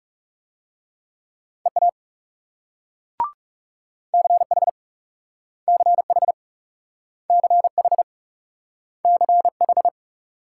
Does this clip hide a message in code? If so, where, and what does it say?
Yes: Morse code from 1.7 to 9.9 s, heard as a keyed tone, about 38 wpm, 700 Hz. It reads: EA CH CH CH CH